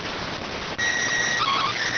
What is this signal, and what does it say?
SSTV (slow-scan television), PD 290.